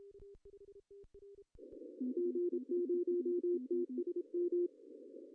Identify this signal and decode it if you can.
Morse code, heard as a keyed tone, about 29 wpm, 350 Hz, with radio noise. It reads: G0TSM
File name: CW / cqww-cw-k5tr